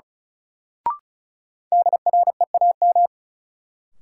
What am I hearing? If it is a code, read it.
Morse code, heard as a keyed tone, about 36 wpm, 700 Hz. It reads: DREAM
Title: Sentences from Top 1500 Words - New Word Drill - 35wpm